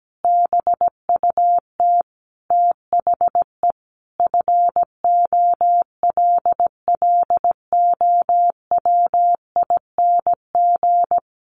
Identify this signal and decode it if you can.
Morse code, heard as a keyed tone, about 17 wpm, 700 Hz. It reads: BUT THE FOLLOWING
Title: Sentences from Top 1500 Words 17wpm